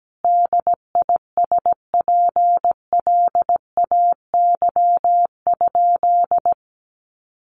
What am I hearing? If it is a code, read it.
Morse code, heard as a keyed tone, about 17 wpm, 700 Hz. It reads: DISPLAY?
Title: Sentences from Top 1200 Words 17wpm